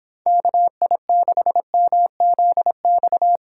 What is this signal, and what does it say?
Morse code, heard as a keyed tone, about 24 wpm, 700 Hz. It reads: KI6MZX